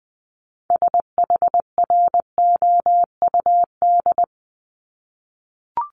Morse code, heard as a keyed tone, about 20 wpm, 700 Hz. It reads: SHROUD
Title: Multi-letter Phonemes 20wpm